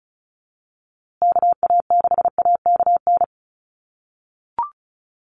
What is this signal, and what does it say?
Morse code, heard as a keyed tone, about 34 wpm, 700 Hz. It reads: KA6AKN
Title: USA Call Signs Repeated 3x 35wpm